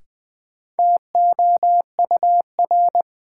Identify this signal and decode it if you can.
Morse code, heard as a keyed tone, about 20 wpm, 700 Hz. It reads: TOUR
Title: Sentences from Top 1400 Words - New Word Drill Alt - 20wpm